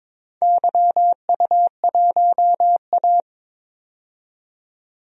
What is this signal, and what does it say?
Morse code, heard as a keyed tone, about 22 wpm, 700 Hz. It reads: YU1A
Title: DXCC-100 Call Signs - 4 Characters - 22wpm